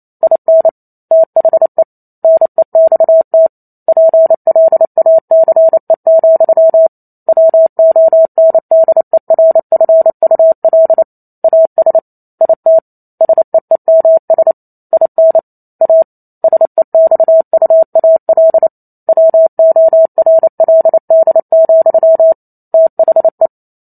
Morse code, heard as a keyed tone, about 28 wpm, 650 Hz. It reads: IN THE NEXT PLACE, WONDERFUL AS IT SEEMS IN A SEXUAL WORLD, THE